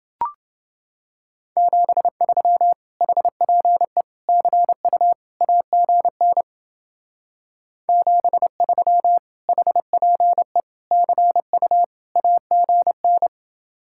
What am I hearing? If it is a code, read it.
Morse code, heard as a keyed tone, about 30 wpm, 700 Hz. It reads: 73 HPE CU AGN 73 HPE CU AGN